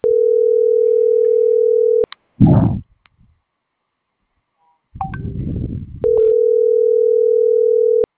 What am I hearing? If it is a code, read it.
Ringback tone (North America), 440 + 480 Hz.